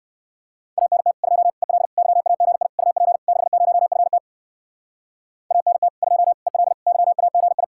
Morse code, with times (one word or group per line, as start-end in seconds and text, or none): 0.8-1.1 ANT
1.2-1.5 2
1.6-1.9 EL
2.0-2.7 YAGI
2.8-3.2 UP
3.3-4.2 60FT
5.5-5.9 ANT
6.0-6.3 2
6.5-6.7 EL
6.9-7.6 YAGI